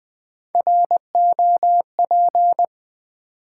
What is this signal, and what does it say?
Morse code, heard as a keyed tone, about 20 wpm, 700 Hz. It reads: ROP
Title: Build A Word - 1 Letter At A Time - 3 to 5-Letter Words - 20wpm